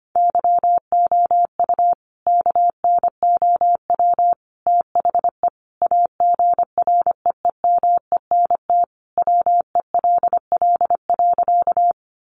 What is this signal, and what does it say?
Morse code, heard as a keyed tone, about 25 wpm, 700 Hz. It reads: YOU KNOW THE AGREEMENT WELL.
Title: Sentences from Top 1500 Words 25wpm